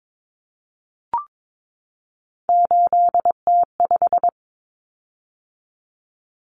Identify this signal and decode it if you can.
Morse code, heard as a keyed tone, about 22 wpm, 700 Hz. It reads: 8T5
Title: Common Cut Numbers - 3-Digit 22wpm